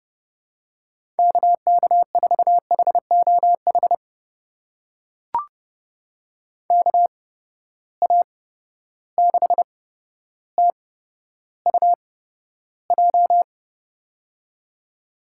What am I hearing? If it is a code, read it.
Morse code, heard as a keyed tone, about 30 wpm, spaced as for 13 wpm, 700 Hz. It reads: KK4HOH KA6TUJ
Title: USA Call Signs - 6 Characters - 8x Character Spacing - 30wpm